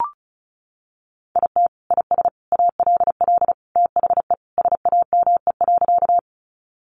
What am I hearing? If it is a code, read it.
Morse code, heard as a keyed tone, about 34 wpm, 700 Hz. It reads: IT IS ALL THE SAME.
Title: Sentences from Top 200 Words 35wpm